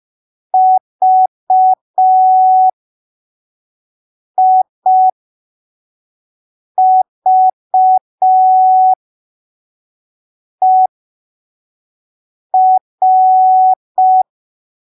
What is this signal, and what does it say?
Morse code, heard as a keyed tone, about 5 wpm, spaced as for 3 wpm, 750 Hz. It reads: VIVER
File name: arquivos / texto / texto 21 05ppm